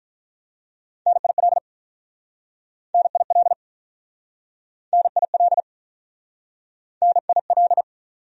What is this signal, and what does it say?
Morse code, heard as a keyed tone, about 48 wpm, 700 Hz. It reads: NIL NIL NIL NIL